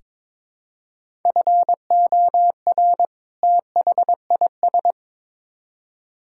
Morse code, heard as a keyed tone, about 22 wpm, 700 Hz. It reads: FOR THIS
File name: Build A Phrase - 2-Word Phrases - 22wpm